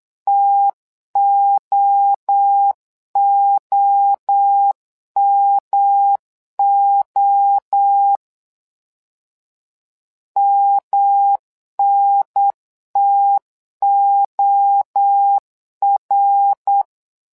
Morse code, heard as a keyed tone, about 8 wpm, 800 Hz. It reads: TOOMO MNTOR
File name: LEZIONI / 3 MWRO / 3 MWRO 8wpm 2606